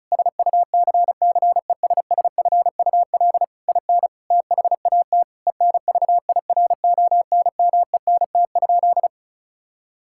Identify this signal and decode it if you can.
Morse code, heard as a keyed tone, about 36 wpm, 700 Hz. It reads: SUCCESSFUL IN THAT ENVIRONMENT?
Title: Sentences from Top 1000 Words - Review Part 5 - 35wpm